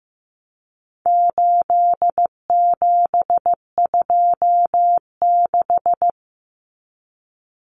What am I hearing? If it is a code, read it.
Morse code, heard as a keyed tone, about 15 wpm, 700 Hz. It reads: 8726
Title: Common Cut Numbers - 4-Digit 15wpm